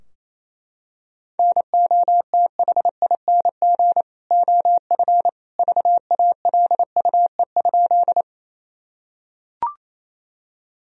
Morse code, heard as a keyed tone, about 28 wpm, 700 Hz. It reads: NOTHING OF VALUE?
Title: Sentences from Top 1000 Words - Review Part 3 - 28wpm